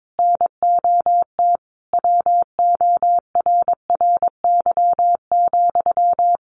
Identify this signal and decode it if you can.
Morse code, heard as a keyed tone, about 22 wpm, 700 Hz. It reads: NOT WORRY,
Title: Sentences from Top 1300 Words 22wpm